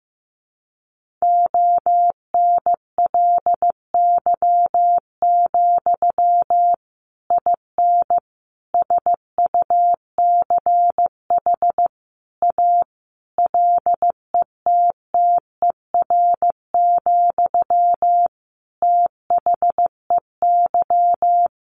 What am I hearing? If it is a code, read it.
Morse code, heard as a keyed tone, about 15 wpm, 700 Hz. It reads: ONLY, IN SUCH A LETTER, THEY
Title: Sentences from Top 1500 Words 15wpm